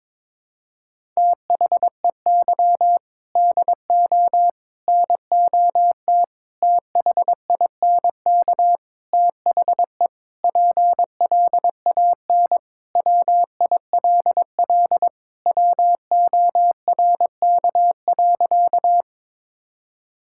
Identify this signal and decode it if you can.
Morse code, heard as a keyed tone, about 22 wpm, 700 Hz. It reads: THEY DO NOT THINK THE PLAN WILL WORK.
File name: practice / 22wpm / Sentences with 4-Word Phrases - 22wpm